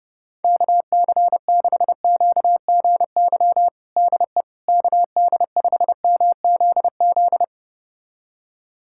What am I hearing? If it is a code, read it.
Morse code, heard as a keyed tone, about 30 wpm, 700 Hz. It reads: KC6QGY DE KD5MZZ
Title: USA Callsign DE Callsign - 6 Characters - 30wpm